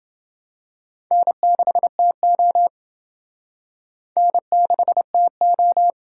Morse code, heard as a keyed tone, about 27 wpm, 700 Hz. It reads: N6TO N6TO